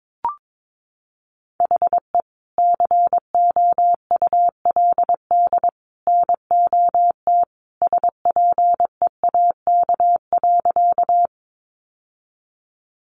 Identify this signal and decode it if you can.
Morse code, heard as a keyed tone, about 22 wpm, 700 Hz. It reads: HE COULD NOT SPEAK.
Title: Sentences from Top 1100 Words 22wpm